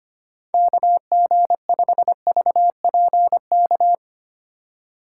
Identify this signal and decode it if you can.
Morse code, heard as a keyed tone, about 25 wpm, 700 Hz. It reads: KG5VPK
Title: USA Call Signs - 2x Character Spacing - 25wpm